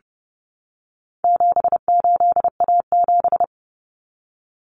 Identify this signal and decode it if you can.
Morse code, heard as a keyed tone, about 30 wpm, 700 Hz. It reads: 78A7
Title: Common Cut Numbers - 4-Digit 30wpm